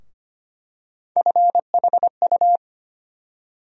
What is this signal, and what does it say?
Morse code, heard as a keyed tone, about 25 wpm, 700 Hz. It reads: FHU
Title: USA Call Sign Postfixes - 3 Letters 25wpm